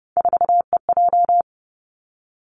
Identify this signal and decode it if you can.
Morse code, heard as a keyed tone, about 30 wpm, 700 Hz. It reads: 4EJ
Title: Three Challenging Letter-Number Combinations 30wpm